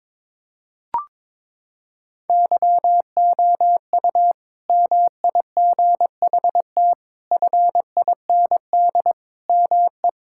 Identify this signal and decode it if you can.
Morse code, heard as a keyed tone, about 22 wpm, 700 Hz. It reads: YOU MIGHT FIND ME
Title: Sentences from Top 1600 Words 22wpm